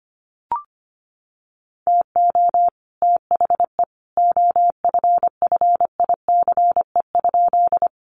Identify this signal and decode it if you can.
Morse code, heard as a keyed tone, about 25 wpm, 700 Hz. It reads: TO THE OFFICE?
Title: Sentences from Top 1000 Words 25wpm